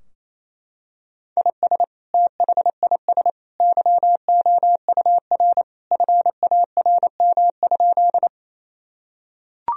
Morse code, heard as a keyed tone, about 28 wpm, 700 Hz. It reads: IS THIS YOUR FARM?
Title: Sentences from Top 1000 Words - Review Part 2 - 28wpm